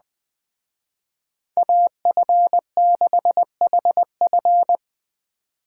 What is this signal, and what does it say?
Morse code, heard as a keyed tone, about 20 wpm, 700 Hz. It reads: AF6HF